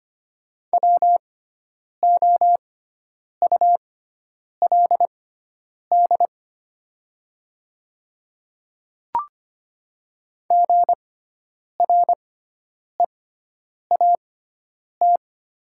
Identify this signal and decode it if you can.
Morse code, heard as a keyed tone, about 25 wpm, spaced as for 9 wpm, 700 Hz. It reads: WOULD GREAT
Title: 6x Character Spacing - Top 200 Words - 25wpm